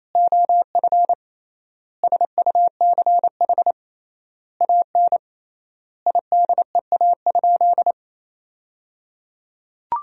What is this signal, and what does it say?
Morse code, heard as a keyed tone, about 28 wpm, 700 Hz. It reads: OF SUCH AN IDEA?
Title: 3x Word-Spacing - Sentences from Top 400 Words - 28wpm